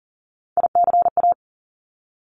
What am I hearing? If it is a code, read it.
Morse code, heard as a keyed tone, about 38 wpm, 700 Hz. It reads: ICA
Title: Top Three Letter Combinations in Words 40wpm